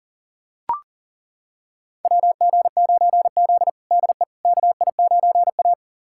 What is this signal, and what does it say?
Morse code, heard as a keyed tone, about 40 wpm, 700 Hz. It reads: WG9Z DE KI9A
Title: USA Callsign DE Callsign - 4 Characters - 40wpm